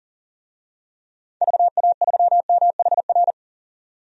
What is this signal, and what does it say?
Morse code, heard as a keyed tone, about 40 wpm, 700 Hz. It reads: VA3MHR
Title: DXCC-100 Call Signs - 6 Characters - 40wpm